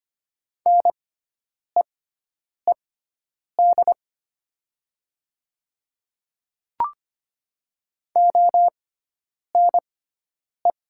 Morse code, heard as a keyed tone, about 25 wpm, spaced as for 9 wpm, 690 Hz. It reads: NEED ONE
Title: 6x Character Spacing - Top 200 Words - 25wpm